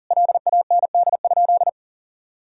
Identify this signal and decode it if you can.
Morse code, heard as a keyed tone, about 40 wpm, 700 Hz. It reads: LAND?
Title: Sentences from Top 1000 Words - Review Part 4 - 40wpm